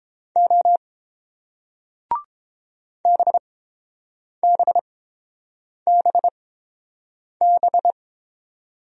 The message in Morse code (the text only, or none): O BBBB